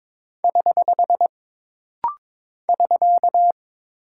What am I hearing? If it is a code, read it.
Morse code, heard as a keyed tone, about 22 wpm, 700 Hz. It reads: <HH> <SK>
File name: practice / 22wpm / Single Punctuation-Prosign - Rapid-Fire - 22wpm